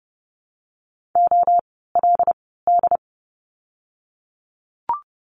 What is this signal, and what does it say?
Morse code, heard as a keyed tone, about 30 wpm, 700 Hz. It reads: O L D
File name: Lesson 16f - Letter c - Words 30wpm